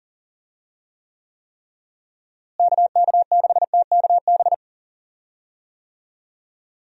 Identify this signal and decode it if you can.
Morse code, heard as a keyed tone, about 40 wpm, 700 Hz. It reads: KK6TKB
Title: USA Call Signs - 6 Characters - Repeated 3x - 40wpm